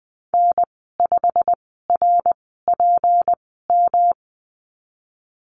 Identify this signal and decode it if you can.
Morse code, heard as a keyed tone, about 20 wpm, spaced as for 15 wpm, 700 Hz. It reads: N5RPM
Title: 213 Lesson 34g - Number 8 - Call Signs 20wpm